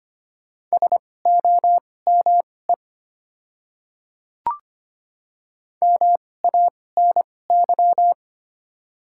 Morse code, heard as a keyed tone, about 25 wpm, spaced as for 18 wpm, 700 Hz. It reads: SOME MANY
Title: 2x Character Spacing - Top 100 Words - 25wpm